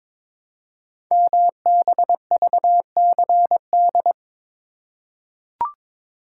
Morse code, heard as a keyed tone, about 22 wpm, 700 Hz. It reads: MBVCD
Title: Code Groups - 5 Letters - 22wpm